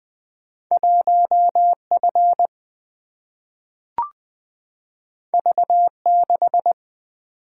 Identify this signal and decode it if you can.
Morse code, heard as a keyed tone, about 20 wpm, 700 Hz. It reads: 1F V6
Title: Single-Letter Single-Number 20wpm